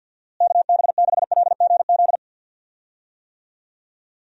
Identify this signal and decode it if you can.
Morse code, heard as a keyed tone, about 51 wpm, 700 Hz. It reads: KB6LGZ
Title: USA Call Signs - 6 Characters - Repeated 2x - 50wpm